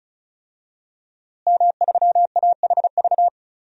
Morse code, heard as a keyed tone, about 34 wpm, 700 Hz. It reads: M3AHV
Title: DXCC-100 Call Signs - 5 Characters - Repeated 2x - 35wpm